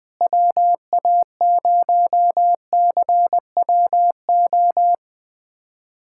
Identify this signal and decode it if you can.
Morse code, heard as a keyed tone, about 20 wpm, 700 Hz. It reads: WA0CWO
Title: USA Callsign DE Callsign - 6 Characters - 20wpm